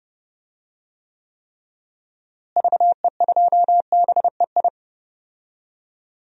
Morse code, heard as a keyed tone, about 32 wpm, 700 Hz. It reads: VE2BEI
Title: DXCC-100 Call Signs - 6 Characters - Repeated 3x - 30wpm